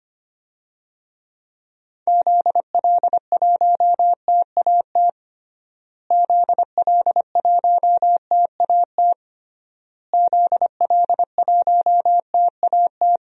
Morse code, heard as a keyed tone, about 25 wpm, 700 Hz. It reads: ZL1TAT ZL1TAT ZL1TAT
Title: DXCC-100 Call Signs - 6 Characters - Repeated 3x - 25wpm